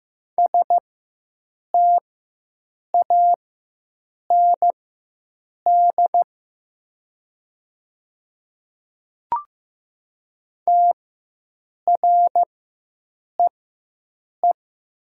Morse code, heard as a keyed tone, about 15 wpm, spaced as for 7 wpm, 700 Hz. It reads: STAND TREE
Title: 4x Character Spacing - Top 300 Words - 15wpm